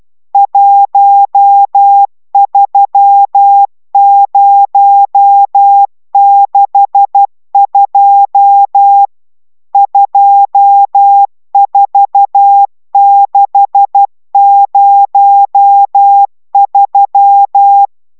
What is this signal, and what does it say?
Morse code, heard as a keyed tone, about 12 wpm, 800 Hz. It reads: 13062 24603
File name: taviro tanulas / cw 69 22 10